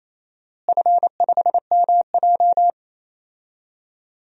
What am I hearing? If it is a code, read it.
Morse code, heard as a keyed tone, about 28 wpm, 700 Hz. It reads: F5MJ
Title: DXCC-100 Call Signs - 4 Characters - Repeated 2x - 28wpm